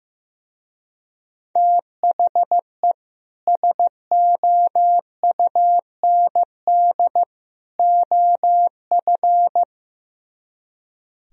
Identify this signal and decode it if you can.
Morse code, heard as a keyed tone, about 15 wpm, 700 Hz. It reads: THE SOUND OF
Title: Speed-Racing - Sets of 3 Words 15wpm